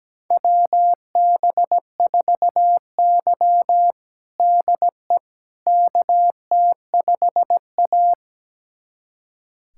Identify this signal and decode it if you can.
Morse code, heard as a keyed tone, about 17 wpm, 700 Hz. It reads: WB4Y DE KT5A
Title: USA Callsign DE Callsign - 4 Characters - 17wpm